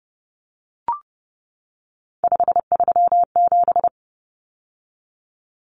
Morse code, heard as a keyed tone, about 30 wpm, 700 Hz. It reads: 537
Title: Numbers - 3-Digit 30wpm